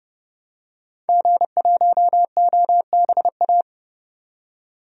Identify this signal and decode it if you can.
Morse code, heard as a keyed tone, about 30 wpm, 690 Hz. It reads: G1OBA
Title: DXCC-100 Call Signs - 5 Characters - 30wpm